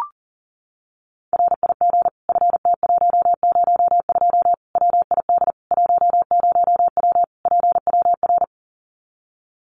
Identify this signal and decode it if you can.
Morse code, heard as a keyed tone, about 37 wpm, 700 Hz. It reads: RIG FT102 WID 10W PWR
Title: Common QSO Phrases 40wpm